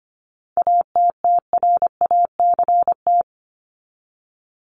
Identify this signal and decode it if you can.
Morse code, heard as a keyed tone, about 25 wpm, 700 Hz. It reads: ATTRACT
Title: Top 1501-1750 Words 25wpm